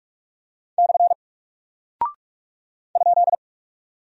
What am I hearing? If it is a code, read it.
Morse code, heard as a keyed tone, about 44 wpm, 700 Hz. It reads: / ?
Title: Single Character (Letter:Number:Punctuation:Prosign) Alt 45wpm